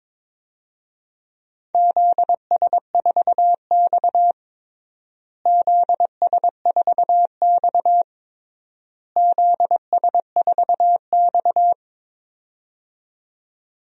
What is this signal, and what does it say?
Morse code, heard as a keyed tone, about 22 wpm, 700 Hz. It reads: ZS4X ZS4X ZS4X
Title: DXCC-100 Call Signs - 4 Characters - Repeated 3x - 22wpm